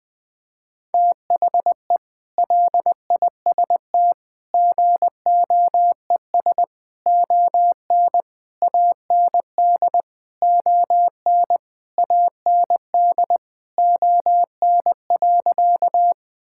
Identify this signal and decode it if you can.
Morse code, heard as a keyed tone, about 20 wpm, 700 Hz. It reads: THE LIST GOES ON AND ON AND ON.
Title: Sentences with 8-Word Phrases - 20wpm